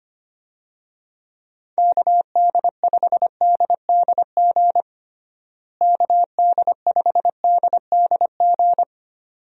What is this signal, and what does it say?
Morse code, heard as a keyed tone, about 25 wpm, 700 Hz. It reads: KD5DDG KD5DDG
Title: USA Call Signs - 6 Characters - Repeated 2x - 25wpm